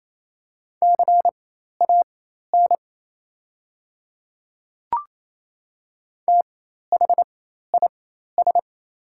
Morse code, heard as a keyed tone, about 28 wpm, spaced as for 13 wpm, 700 Hz. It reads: CAN THIS
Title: 4x Character Spacing - Top 500 Words - 28wpm